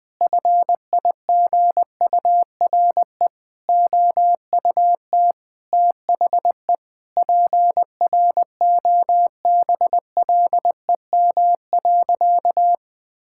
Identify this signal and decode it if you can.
Morse code, heard as a keyed tone, about 20 wpm, 700 Hz. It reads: FIGURE OUT THE PROBLEM.
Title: Sentences with 4-Word Phrases - 20wpm